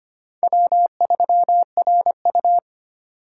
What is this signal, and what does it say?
Morse code, heard as a keyed tone, about 25 wpm, 700 Hz. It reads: W3RU
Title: USA Call Signs - 4 Characters - 6x Character Spacing - 25wpm